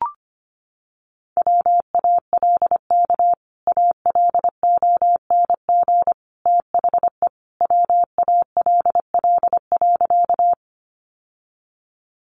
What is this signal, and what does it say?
Morse code, heard as a keyed tone, about 25 wpm, 700 Hz. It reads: WALK ALONG THE WALL.
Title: Sentences from Top 1100 Words 25wpm